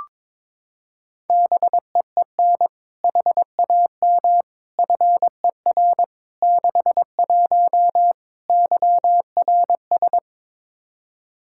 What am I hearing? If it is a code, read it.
Morse code, heard as a keyed tone, about 22 wpm, 700 Hz. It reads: BEEN HAM FER 61 YRS
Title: Practice QSO - Exchange 3 - More Info - 22wpm